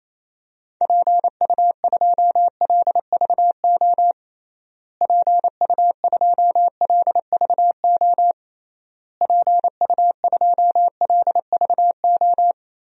Morse code, heard as a keyed tone, about 27 wpm, 700 Hz. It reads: PU2LVO PU2LVO PU2LVO
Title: DXCC-100 Call Signs - 6 Characters - Repeated 3x - 28wpm